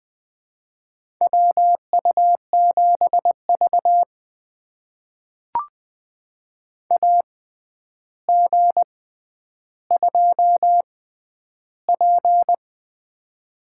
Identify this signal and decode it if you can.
Morse code, heard as a keyed tone, about 20 wpm, spaced as for 10 wpm, 700 Hz. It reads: WU7V AG2P
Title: USA Call Signs - 4 Characters - 6x Character Spacing - 20wpm